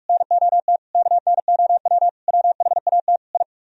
Morse code, heard as a keyed tone, about 44 wpm, 700 Hz. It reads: NOT KNOW WHAT I